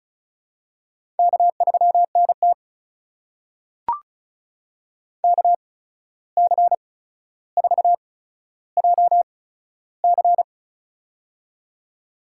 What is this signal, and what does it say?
Morse code, heard as a keyed tone, about 36 wpm, spaced as for 14 wpm, 700 Hz. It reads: K3NT KC4JC
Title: USA Call Signs - 8x Character Spacing - 35wpm